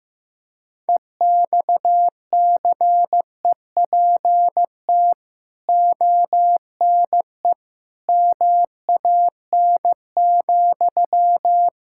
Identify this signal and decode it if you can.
Morse code, heard as a keyed tone, about 15 wpm, 700 Hz. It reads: EXCEPT ONE MAN,
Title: Sentences from Top 1600 Words 15wpm